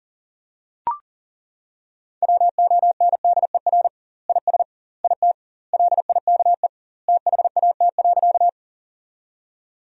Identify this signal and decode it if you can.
Morse code, heard as a keyed tone, about 40 wpm, 700 Hz. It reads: WONDER IS IT LIKE THAT.
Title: 2x Word-Spacing - Sentences from Top 500 Words - 40wpm